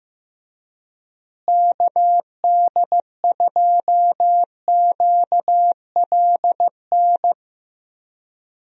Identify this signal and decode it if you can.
Morse code, heard as a keyed tone, about 15 wpm, 700 Hz. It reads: KD2QLN
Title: USA Call Signs - 6 Characters - Repeated 2x - 15wpm